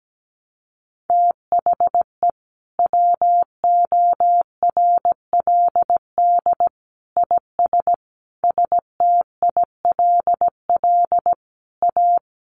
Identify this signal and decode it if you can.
Morse code, heard as a keyed tone, about 17 wpm, 700 Hz. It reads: THE WORLD IS STILL A
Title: Sentences from Top 1200 Words 17wpm